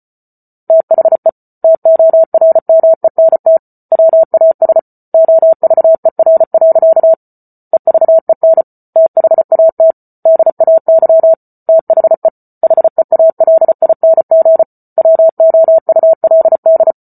Morse code, heard as a keyed tone, about 34 wpm, 650 Hz. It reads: THE TORMENT WAS OVER. EVEN THAT DAY THE HEALING WOULD